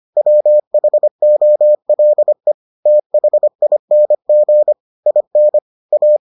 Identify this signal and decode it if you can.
Morse code, heard as a keyed tone, about 25 wpm, 600 Hz. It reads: WHOLE THING IN A